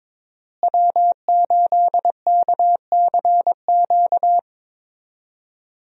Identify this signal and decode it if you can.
Morse code, heard as a keyed tone, about 22 wpm, 700 Hz. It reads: W8KCQ
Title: USA Call Signs - 5 Characters - Repeated 2x - 22wpm